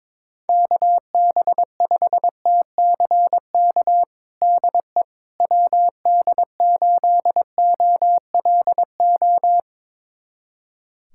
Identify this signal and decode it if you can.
Morse code, heard as a keyed tone, about 22 wpm, 700 Hz. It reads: KB5TCK DE WD8OLO
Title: USA Callsign DE Callsign - 6 Characters - 22wpm